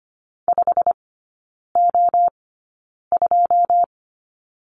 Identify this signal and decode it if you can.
Morse code, heard as a keyed tone, about 25 wpm, spaced as for 9 wpm, 700 Hz. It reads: 5O2
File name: Lesson 17e - Letter u - Three Characters 25wpm